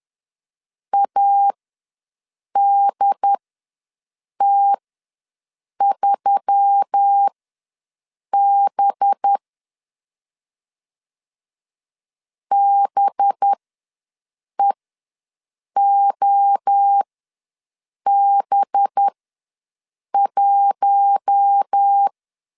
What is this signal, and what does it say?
Morse code, heard as a keyed tone, about 10 wpm, spaced as for 6 wpm, 800 Hz. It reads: ADT3B BEOB1
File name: LEZIONI / 5 LBVC / 5 LB ws 10wpm 3616